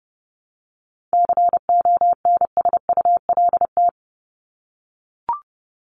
Morse code, heard as a keyed tone, about 30 wpm, 700 Hz. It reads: CONSULT
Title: Top 1751-2000 Words 30wpm